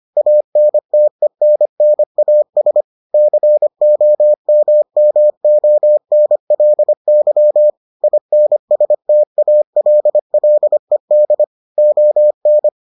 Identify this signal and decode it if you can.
Morse code, heard as a keyed tone, about 25 wpm, 600 Hz. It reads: ANTENNAS COMMONLY INSTALLED ON